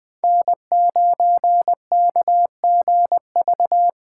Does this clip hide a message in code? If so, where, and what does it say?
Yes: Morse code from 0.2 to 3.9 s, heard as a keyed tone, about 19 wpm, 700 Hz. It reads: N9KGV